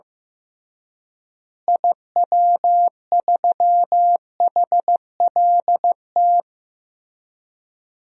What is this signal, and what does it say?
Morse code, heard as a keyed tone, about 15 wpm, 700 Hz. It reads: IW3HLT
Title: DXCC-100 Call Signs - 6 Characters - Repeated 3x - 15wpm